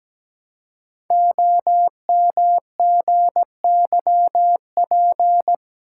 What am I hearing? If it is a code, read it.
Morse code, heard as a keyed tone, about 17 wpm, 700 Hz. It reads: OMGYP
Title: Code Groups - 5 Letters - 17wpm